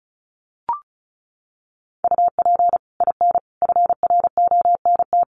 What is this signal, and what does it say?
Morse code, heard as a keyed tone, about 34 wpm, 700 Hz. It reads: UP IN FRONT
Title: Sets of 5 Words Encore2 35wpm